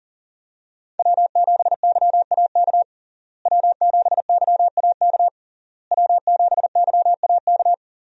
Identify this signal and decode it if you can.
Morse code, heard as a keyed tone, about 37 wpm, 700 Hz. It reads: W7YAK W7YAK W7YAK
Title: USA Call Signs - 5 Characters - Repeated 3x - 40wpm